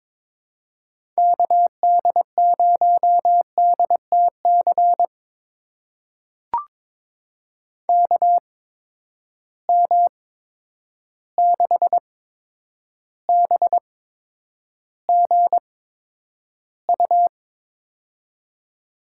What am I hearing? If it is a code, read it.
Morse code, heard as a keyed tone, about 22 wpm, spaced as for 9 wpm, 700 Hz. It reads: KD0DTC KM6BGU
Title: USA Call Signs - 6 Characters - 8x Character Spacing - 22wpm